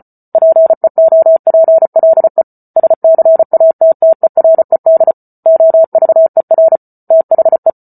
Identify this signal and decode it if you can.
Morse code, heard as a keyed tone, about 34 wpm, 650 Hz. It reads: PEOPLE SCATTERED OVER THE